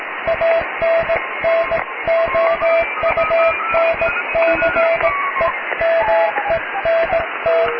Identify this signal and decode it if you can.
Morse code, heard as a keyed tone, about 18 wpm, 660 Hz, with radio noise. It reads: ANNOUNCEMENT
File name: cw / tekst(b)+QRM